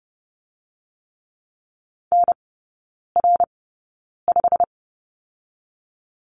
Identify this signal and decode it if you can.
Morse code, heard as a keyed tone, about 30 wpm, 700 Hz. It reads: N R 5
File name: Lesson 19e - Letter w - Three Characters 30wpm